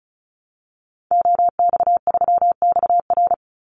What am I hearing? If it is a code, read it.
Morse code, heard as a keyed tone, about 34 wpm, 700 Hz. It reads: OX3XR
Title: Call Signs Repeated 3x - 35wpm